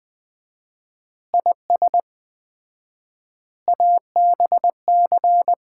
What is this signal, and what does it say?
Morse code, heard as a keyed tone, about 20 wpm, 700 Hz. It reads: IS ABC
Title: 4x Word-Spacing - Sets of 5 Words - 20wpm